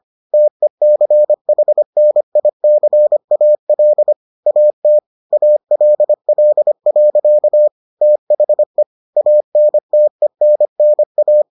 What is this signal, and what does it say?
Morse code, heard as a keyed tone, about 25 wpm, 600 Hz. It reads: TECHNICAL AT ALL. THE ANTENNA